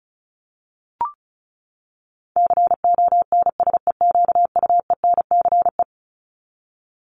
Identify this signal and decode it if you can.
Morse code, heard as a keyed tone, about 34 wpm, 700 Hz. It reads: CONSEQUENCE
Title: LY Suffix - Top 100 Words with Base Words 35wpm